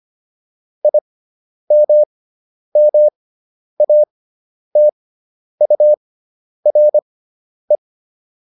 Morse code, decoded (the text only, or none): IMMATURE